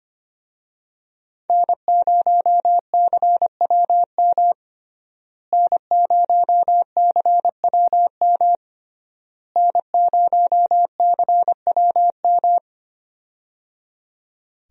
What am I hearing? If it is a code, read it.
Morse code, heard as a keyed tone, about 25 wpm, 700 Hz. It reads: N0CWM N0CWM N0CWM
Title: USA Call Signs - 5 Characters - Repeated 3x - 25wpm